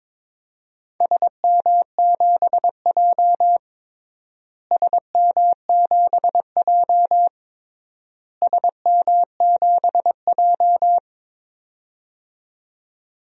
Morse code, heard as a keyed tone, about 22 wpm, 700 Hz. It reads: SM7J SM7J SM7J
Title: DXCC-100 Call Signs - 4 Characters - Repeated 3x - 22wpm